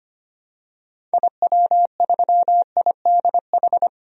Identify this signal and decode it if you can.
Morse code, heard as a keyed tone, about 25 wpm, 700 Hz. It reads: IW3IDH
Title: DXCC-100 Call Signs - 6 Characters - 25wpm